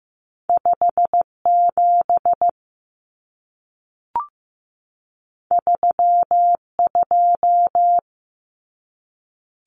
Morse code, heard as a keyed tone, about 15 wpm, 700 Hz. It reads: 5 7 3 2
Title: Common Cut Numbers - 2-Digit 15wpm